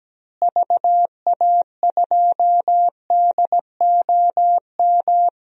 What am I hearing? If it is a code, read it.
Morse code, heard as a keyed tone, about 17 wpm, 700 Hz. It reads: VA2DOM
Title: DXCC-100 Call Signs - 6 Characters - Repeated 3x - 17wpm